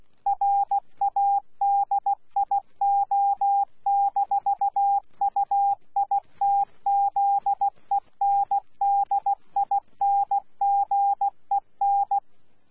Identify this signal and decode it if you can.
Morse code, heard as a keyed tone, about 16 wpm, 800 Hz. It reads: RADIO-UITZENDINGEN
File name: Z PI4VRZ / PI4VRZ 14-jun-2025